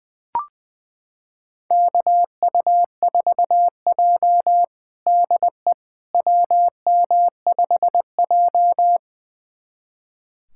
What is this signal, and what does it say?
Morse code, heard as a keyed tone, about 20 wpm, 700 Hz. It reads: KU4J DE WM5J
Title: USA Callsign DE Callsign - 4 Characters - 20wpm